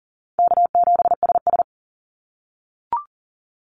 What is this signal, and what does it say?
Morse code, heard as a keyed tone, about 40 wpm, 700 Hz. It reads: K7SS
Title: Easy Call Signs - 40wpm